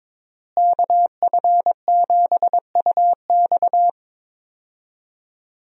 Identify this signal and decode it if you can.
Morse code, heard as a keyed tone, about 22 wpm, 700 Hz. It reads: KF7UX
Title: USA Call Signs - 5 Characters - Repeated 3x - 22wpm